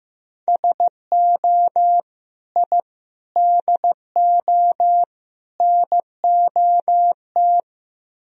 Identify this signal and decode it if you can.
Morse code, heard as a keyed tone, about 17 wpm, 700 Hz. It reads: SO I DO NOT
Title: Speed-Racing - Sets of 4 Words 15wpm